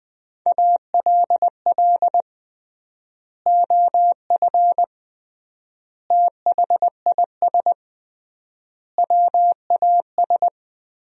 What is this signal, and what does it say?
Morse code, heard as a keyed tone, about 20 wpm, 700 Hz. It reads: ALL OF THIS WAS
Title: 3x Word-Spacing - Sentences from Top 500 Words - 20wpm